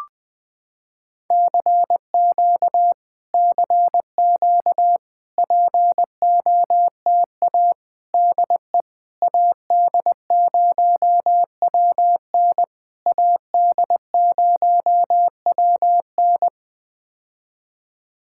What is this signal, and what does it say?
Morse code, heard as a keyed tone, about 20 wpm, 700 Hz. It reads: CQ CQ POTA DE AD0WN AD0WN
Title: POTA QSO Practice - 20wpm